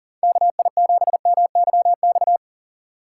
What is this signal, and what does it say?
Morse code, heard as a keyed tone, about 40 wpm, 700 Hz. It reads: KI7MYX
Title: USA Call Signs - 6 Characters - 6x Character Spacing - 40wpm